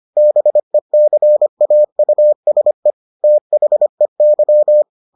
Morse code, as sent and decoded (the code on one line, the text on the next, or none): -... -.-. .- ..- ... / - .... -.--
BECAUSE THEY